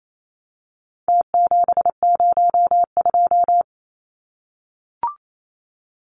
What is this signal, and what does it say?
Morse code, heard as a keyed tone, about 28 wpm, 700 Hz. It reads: T702
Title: Numbers - Contest Style - 4-Digit 28wpm